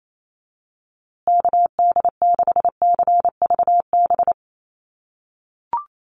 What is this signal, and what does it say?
Morse code, heard as a keyed tone, about 28 wpm, 700 Hz. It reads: KD6CVB
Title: USA Call Signs Repeated 3x 28wpm